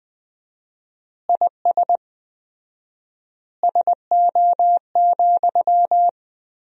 Morse code, heard as a keyed tone, about 20 wpm, 700 Hz. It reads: IS SO,
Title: 4x Word-Spacing - Sentences from Top 200 Words - 20wpm